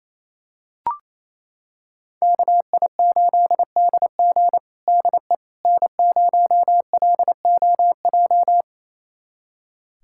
Morse code, heard as a keyed tone, about 28 wpm, 700 Hz. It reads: KI8DG DE N0LOJ
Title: USA Callsign DE Callsign - 5 Characters - 28wpm